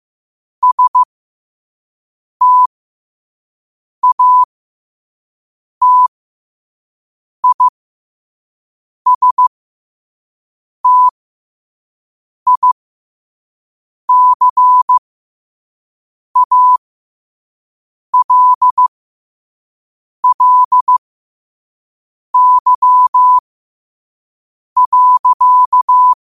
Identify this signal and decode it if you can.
Morse code, heard as a keyed tone, about 15 wpm, spaced as for 5 wpm, 1000 Hz. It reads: STATISTICALLY.
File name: qotdfiles / 5wpm